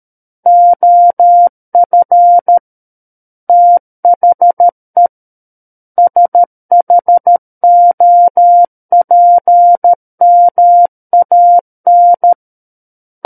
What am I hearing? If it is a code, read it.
Morse code, heard as a keyed tone, about 13 wpm, 700 Hz. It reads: OF THE SHOPMAN
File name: Book One / 04-Chapter 4 - 13 WPM